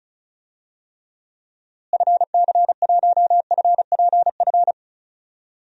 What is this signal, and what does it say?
Morse code, heard as a keyed tone, about 34 wpm, 700 Hz. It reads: FC1FPF
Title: DXCC-100 Call Signs - 6 Characters - Repeated 3x - 35wpm